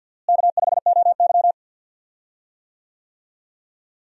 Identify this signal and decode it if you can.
Morse code, heard as a keyed tone, about 51 wpm, 700 Hz. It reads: K5OY
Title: USA Call Signs - 4 Characters - Repeated 2x - 50wpm